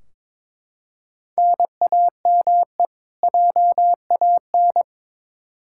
Morse code, heard as a keyed tone, about 22 wpm, 700 Hz. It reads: NAME JAN
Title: Practice QSO - Exchange 1 - Beginning - 22wpm